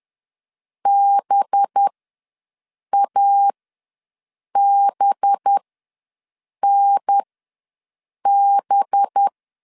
Morse code, heard as a keyed tone, about 10 wpm, spaced as for 6 wpm, 800 Hz. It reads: BABNB
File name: LEZIONI / 5 LBVC / 5 LB ws 10wpm 3616